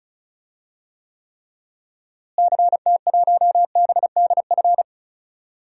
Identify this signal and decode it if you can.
Morse code, heard as a keyed tone, about 36 wpm, 700 Hz. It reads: CT1BDF
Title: DXCC-100 Call Signs - 6 Characters - Repeated 3x - 35wpm